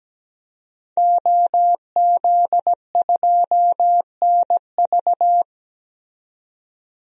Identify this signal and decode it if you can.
Morse code, heard as a keyed tone, about 17 wpm, 700 Hz. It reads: OZ2NV
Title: DXCC-100 Call Signs - 5 Characters - 17wpm